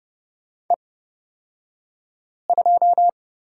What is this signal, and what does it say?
Morse code, heard as a keyed tone, about 30 wpm, spaced as for 5 wpm, 700 Hz. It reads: E2
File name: Lesson 36d - BT - Two Characters 30wpm